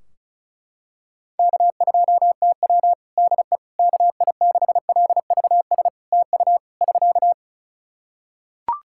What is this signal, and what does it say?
Morse code, heard as a keyed tone, about 36 wpm, 700 Hz. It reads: K2TW DE KI6LVS TU <SK>
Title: Practice QSO - Exchange 4 - Ending - 35wpm